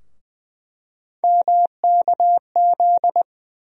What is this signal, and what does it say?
Morse code, heard as a keyed tone, about 20 wpm, 700 Hz. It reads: MKZ
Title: USA Call Sign Postfixes - 3 Letters 20wpm